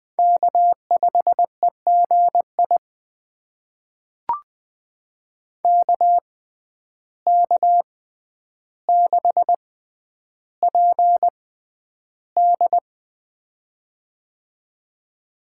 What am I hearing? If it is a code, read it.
Morse code, heard as a keyed tone, about 20 wpm, spaced as for 10 wpm, 700 Hz. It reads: K5EGI KK6PD
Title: USA Call Signs - 5 Characters - 6x Character Spacing - 20wpm